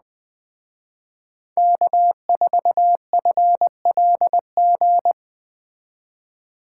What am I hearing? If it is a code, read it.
Morse code, heard as a keyed tone, about 20 wpm, 700 Hz. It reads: K4FLG
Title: Speed Racing - USA 1x3 Call Signs 17wpm